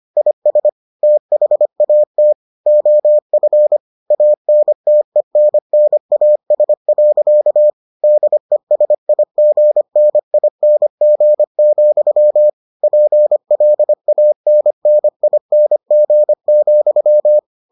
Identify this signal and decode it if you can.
Morse code, heard as a keyed tone, about 25 wpm, 600 Hz. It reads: IS THAT OF ANTENNAS. DESIGNING, PLANNING,